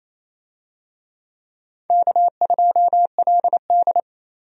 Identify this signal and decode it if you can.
Morse code, heard as a keyed tone, about 27 wpm, 700 Hz. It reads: K2LD
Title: USA Call Signs - 4 Characters - 28wpm